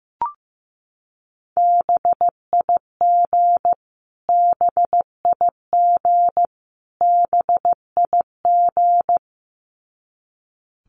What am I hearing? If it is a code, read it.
Morse code, heard as a keyed tone, about 14 wpm, 700 Hz. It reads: BIG BIG BIG
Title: Top 200 Words Repeated 15wpm